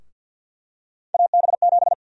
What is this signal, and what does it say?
Morse code, heard as a keyed tone, about 53 wpm, 700 Hz. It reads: AB7
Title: USA Call Sign Prefixes - 50wpm